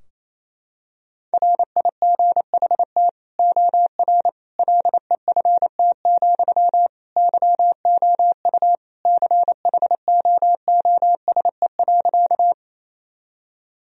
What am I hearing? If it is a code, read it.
Morse code, heard as a keyed tone, about 27 wpm, 700 Hz. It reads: RIGHT OR LEFT, YOU CHOOSE.
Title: Sentences from Top 1000 Words - Review Part 4 - 28wpm